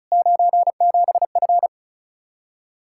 Morse code, heard as a keyed tone, about 45 wpm, 700 Hz. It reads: 97F